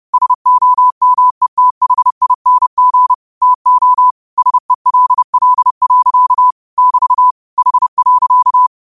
Morse code, heard as a keyed tone, about 30 wpm, 1000 Hz. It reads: SOMETHING TO SELL. <BT> H.